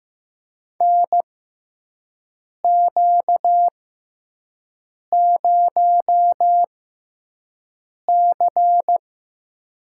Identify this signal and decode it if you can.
Morse code, heard as a keyed tone, about 15 wpm, spaced as for 5 wpm, 700 Hz. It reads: NQ0C
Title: USA Call Signs - 4 Characters - 6x Character Spacing - 15wpm